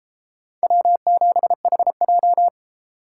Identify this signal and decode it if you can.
Morse code, heard as a keyed tone, about 33 wpm, 700 Hz. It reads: W7HJ